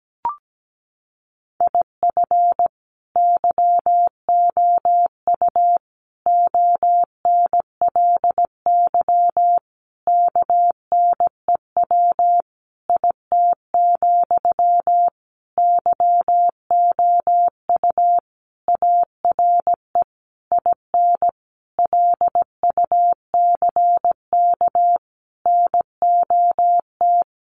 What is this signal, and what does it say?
Morse code, heard as a keyed tone, about 17 wpm, 700 Hz. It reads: IF YOU ONLY KNEW IT, YOU ARE IN LUCK NOT
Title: Sentences from Top 1500 Words 17wpm